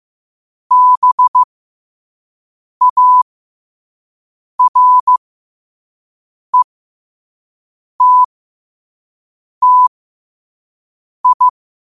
Morse code, heard as a keyed tone, about 15 wpm, spaced as for 5 wpm, 1000 Hz. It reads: BARETTI
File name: qotdfiles / 5wpm